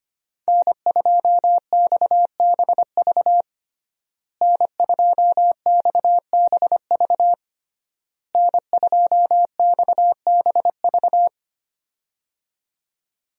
Morse code, heard as a keyed tone, about 25 wpm, 700 Hz. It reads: N2XBV N2XBV N2XBV
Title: USA Call Signs - 5 Characters - Repeated 3x - 25wpm